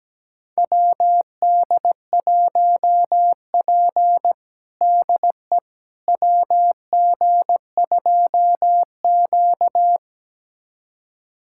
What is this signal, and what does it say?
Morse code, heard as a keyed tone, about 17 wpm, 700 Hz. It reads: WD1P DE WG2Q
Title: USA Callsign DE Callsign - 4 Characters - 17wpm